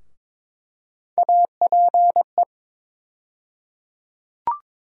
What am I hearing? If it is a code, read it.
Morse code, heard as a keyed tone, about 22 wpm, 700 Hz. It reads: APE
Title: USA Call Sign Postfixes - 3 Letters 22wpm